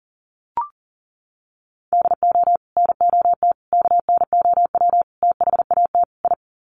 Morse code, heard as a keyed tone, about 40 wpm, 700 Hz. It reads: DO NOT KNOW THAT I
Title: Sets of 5 Words Encore2 40wpm